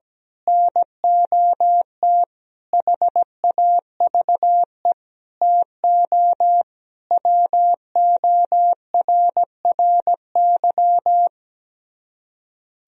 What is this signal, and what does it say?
Morse code, heard as a keyed tone, about 17 wpm, 700 Hz. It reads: NOT HAVE TO WORRY
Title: Speed-Racing - Sets of 5 Words 17wpm